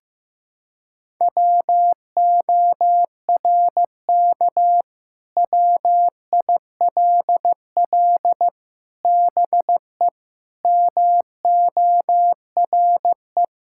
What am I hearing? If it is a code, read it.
Morse code, heard as a keyed tone, about 16 wpm, 700 Hz. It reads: WORK WILL BE MORE
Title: Sentences from Top 1000 Words - Review Part 6 - 15wpm